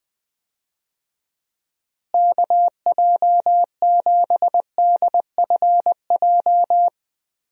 Morse code, heard as a keyed tone, about 21 wpm, 700 Hz. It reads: KJ7DFJ
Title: USA Call Signs - 6 Characters - Repeated 2x - 20wpm